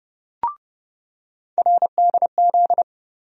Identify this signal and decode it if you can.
Morse code, heard as a keyed tone, about 30 wpm, 700 Hz. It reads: RDZ
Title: USA Call Sign Postfixes - 3 Letters 30wpm